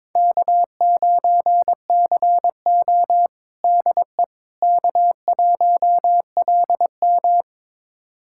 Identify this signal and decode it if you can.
Morse code, heard as a keyed tone, about 21 wpm, 700 Hz. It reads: K9CO DE K1LM
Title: USA Callsign DE Callsign - 4 Characters - 22wpm